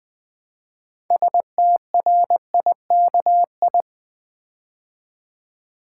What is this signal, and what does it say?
Morse code, heard as a keyed tone, about 20 wpm, 700 Hz. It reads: STRIKI
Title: Build A Word - 3 Letters At A Time - 8-Letter Words - 20wpm